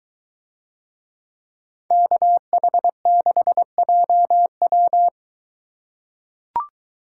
Morse code, heard as a keyed tone, about 22 wpm, 700 Hz. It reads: KH6JW